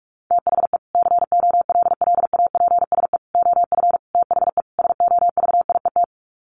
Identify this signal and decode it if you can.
Morse code, heard as a keyed tone, about 44 wpm, 700 Hz. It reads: THE COLLAPSE OF THE SOVIET
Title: Sets of 5 Words Encore 45wpm